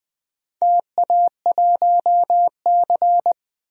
Morse code, heard as a keyed tone, about 20 wpm, 700 Hz. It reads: TA1C
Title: DXCC-100 Call Signs - 4 Characters - Repeated 3x - 20wpm